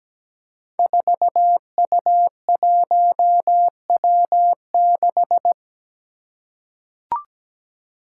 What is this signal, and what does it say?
Morse code, heard as a keyed tone, about 17 wpm, 700 Hz. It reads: 4U1W6
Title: Five Challenging Letter-Number Combinations 17wpm